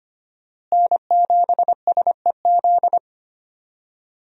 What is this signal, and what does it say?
Morse code, heard as a keyed tone, about 25 wpm, 700 Hz. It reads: N7SEZ
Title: USA Call Signs - 5 Characters - Repeated 2x - 25wpm